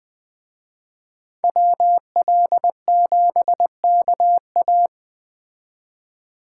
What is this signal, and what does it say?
Morse code, heard as a keyed tone, about 20 wpm, 700 Hz. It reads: WL7KA
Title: Speed Racing - USA Call Signs 17wpm